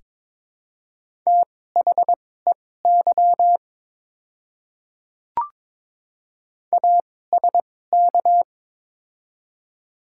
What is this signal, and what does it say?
Morse code, heard as a keyed tone, about 22 wpm, spaced as for 16 wpm, 700 Hz. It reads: THEY ASK
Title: 2x Character Spacing - Top 200 Words - 22wpm